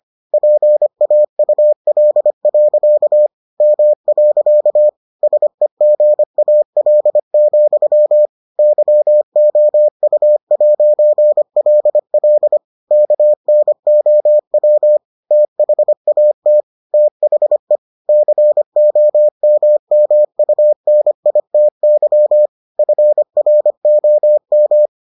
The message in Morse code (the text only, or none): PAUL. M. SEGAL, YOU'LL KNOW THAT THE COMMUNITY FROM